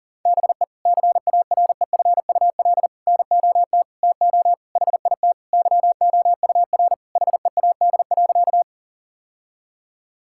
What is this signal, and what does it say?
Morse code, heard as a keyed tone, about 40 wpm, 700 Hz. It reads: BE CAREFUL NOT TO HIT YOUR HEAD.
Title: Sentences from Top 1600 Words 40wpm